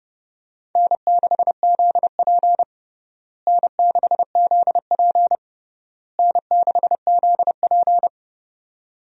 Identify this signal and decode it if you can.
Morse code, heard as a keyed tone, about 31 wpm, 700 Hz. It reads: N6ZP N6ZP N6ZP
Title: USA Call Signs - 4 Characters - Repeated 3x - 30wpm